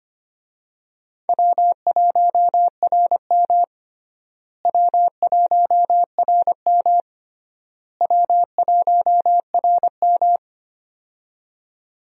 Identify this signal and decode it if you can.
Morse code, heard as a keyed tone, about 25 wpm, 700 Hz. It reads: W1RM W1RM W1RM
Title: USA Call Signs - 4 Characters - Repeated 3x - 25wpm